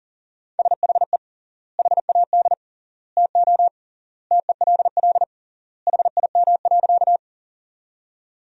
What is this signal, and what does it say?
Morse code, heard as a keyed tone, about 40 wpm, 700 Hz. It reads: SHE HAD TO TELL HIM.
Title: 3x Word-Spacing - Sentences from Top 200 Words - 40wpm